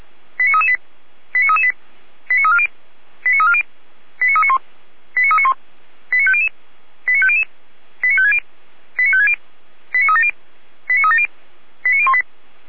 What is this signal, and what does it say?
Five-tone selective call (ZVEI), heard as quick runs of five tones, tones of about 70 ms. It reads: 89338 89338 89340 89340 89391 89391 89500 89500 89560 89560 89370 89370 89018
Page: SDR/ZVEI